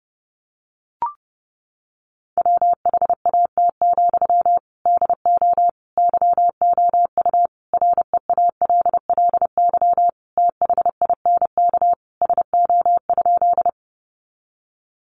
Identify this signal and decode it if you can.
Morse code, heard as a keyed tone, about 30 wpm, 700 Hz. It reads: WHAT, DO YOU REALLY THINK SO?
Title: Sentences from Top 600 Words 30wpm